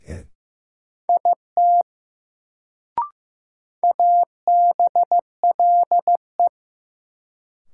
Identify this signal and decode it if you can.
Morse code, heard as a keyed tone, about 16 wpm, 700 Hz. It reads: IT ABLE
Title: Sets of 2 Words - New Word Drill - 15wpm